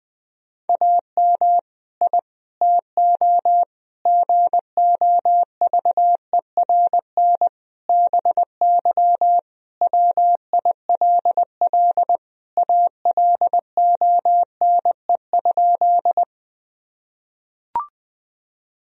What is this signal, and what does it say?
Morse code, heard as a keyed tone, about 21 wpm, 700 Hz. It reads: AM I TO GOVERN BY WILL ALONE?
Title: Sentences from Top 1000 Words - Review Part 2 - 20wpm